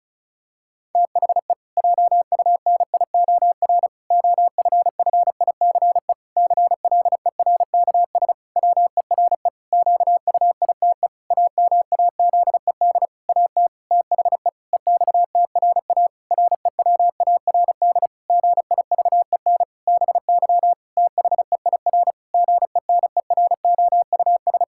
Morse code, heard as a keyed tone, about 36 wpm, 700 Hz. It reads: THE JUNIOR OFFICE CLERKS WERE QUITE AMAZED AT THE EXTRA REWARD GIVEN BY THEIR GENEROUS
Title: Pangram Challenge v2 - 35wpm